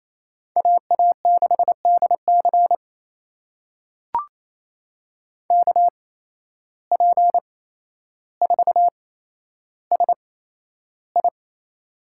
Morse code, heard as a keyed tone, about 28 wpm, spaced as for 12 wpm, 700 Hz. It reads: AA6DC KP4SI
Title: USA Call Signs - 8x Character Spacing - 28wpm